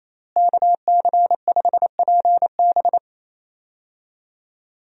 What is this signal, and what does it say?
Morse code, heard as a keyed tone, about 28 wpm, 700 Hz. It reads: KC5PB
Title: USA Call Signs - 5 Characters - Repeated 3x - 28wpm